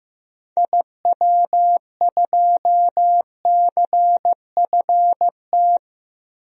Morse code, heard as a keyed tone, about 15 wpm, 700 Hz. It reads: IW2CFT
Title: DXCC-100 Call Signs - 6 Characters - 15wpm